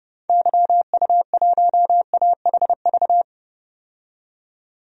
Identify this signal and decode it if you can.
Morse code, heard as a keyed tone, about 30 wpm, 700 Hz. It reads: YU1AHV
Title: DXCC-100 Call Signs - 6 Characters - 30wpm